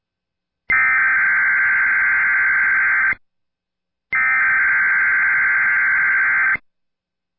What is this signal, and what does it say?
Emergency Alert System (SAME) tones (header bursts), heard as harsh buzzing bursts. It is heard from 0.7 to 7.0 s.